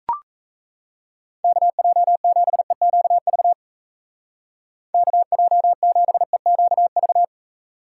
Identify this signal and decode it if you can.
Morse code, heard as a keyed tone, about 40 wpm, 700 Hz. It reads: KJ7EQV KJ7EQV